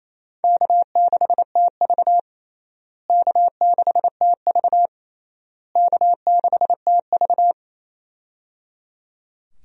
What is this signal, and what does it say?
Morse code, heard as a keyed tone, about 28 wpm, 700 Hz. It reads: K6TV K6TV K6TV
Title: USA Call Signs - 4 Characters - Repeated 3x - 28wpm